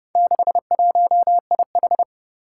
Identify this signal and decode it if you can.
Morse code, heard as a keyed tone, about 30 wpm, 700 Hz. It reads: 61IH
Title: Four Challenging Letter-Number Combinations 30wpm